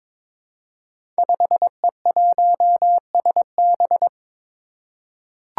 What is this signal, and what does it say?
Morse code, heard as a keyed tone, about 22 wpm, 700 Hz. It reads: 5E1SB
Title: Five Challenging Letter-Number Combinations 22wpm